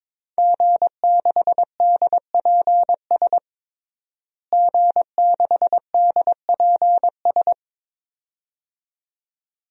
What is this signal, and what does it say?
Morse code, heard as a keyed tone, about 22 wpm, 700 Hz. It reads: G6DPS G6DPS
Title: DXCC-100 Call Signs - 5 Characters - Repeated 2x - 22wpm